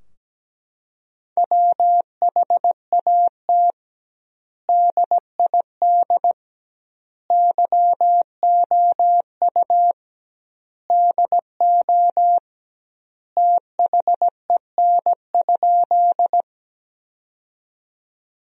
Morse code, heard as a keyed tone, about 17 wpm, 700 Hz. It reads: WHAT DID YOU DO THEN?
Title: 2x Word-Spacing - Sentences from Top 100 Words - 17wpm